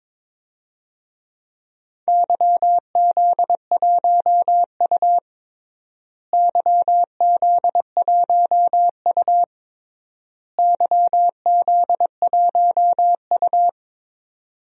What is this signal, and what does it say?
Morse code, heard as a keyed tone, about 22 wpm, 700 Hz. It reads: YZ1U YZ1U YZ1U
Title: DXCC-100 Call Signs - 4 Characters - Repeated 3x - 22wpm